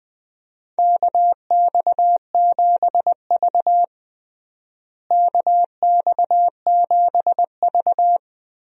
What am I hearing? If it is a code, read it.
Morse code, heard as a keyed tone, about 20 wpm, 700 Hz. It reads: KX7V KX7V
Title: USA Call Signs - 4 Characters - Repeated 3x - 20wpm